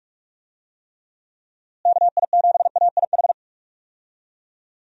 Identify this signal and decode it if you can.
Morse code, heard as a keyed tone, about 46 wpm, 700 Hz. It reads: KI7AIH
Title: USA Call Signs - 6 Characters - Repeated 3x - 45wpm